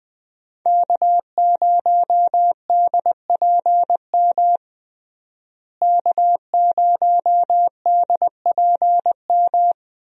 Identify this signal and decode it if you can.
Morse code, heard as a keyed tone, about 20 wpm, 700 Hz. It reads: K0DPM K0DPM
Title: USA Call Signs - 5 Characters - Repeated 2x - 20wpm